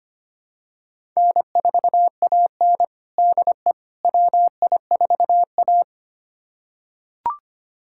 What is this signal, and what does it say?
Morse code, heard as a keyed tone, about 25 wpm, 700 Hz. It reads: N4AN DE WI4A
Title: USA Callsign DE Callsign - 4 Characters - 25wpm